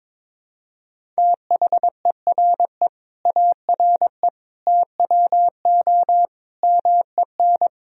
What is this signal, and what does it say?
Morse code, heard as a keyed tone, about 21 wpm, 700 Hz. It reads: THERE ARE TWO MEN
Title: Sentences from Top 1000 Words - Review Part 3 - 22wpm